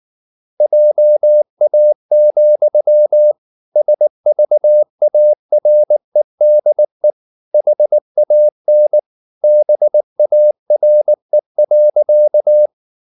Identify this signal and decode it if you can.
Morse code, heard as a keyed tone, about 19 wpm, 600 Hz. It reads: JA, SVAREDE HAN BARE.
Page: Ferie 19 wpm